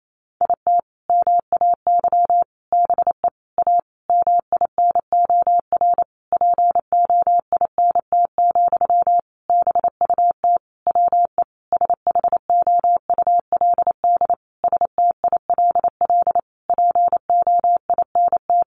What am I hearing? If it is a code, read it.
Morse code, heard as a keyed tone, about 28 wpm, 700 Hz. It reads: IT MAY BE A MINOR POINT, BUT WE SHOULD STILL POINT
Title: Sentences from Top 1500 Words 28wpm